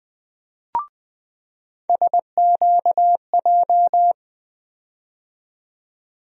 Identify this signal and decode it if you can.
Morse code, heard as a keyed tone, about 20 wpm, 700 Hz. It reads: SQJ
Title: USA Call Sign Postfixes - 3 Letters 20wpm